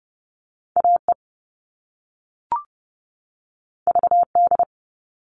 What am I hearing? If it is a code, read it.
Morse code, heard as a keyed tone, about 30 wpm, 700 Hz. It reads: AE VD
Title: All Cut Numbers - 2-Digit 30wpm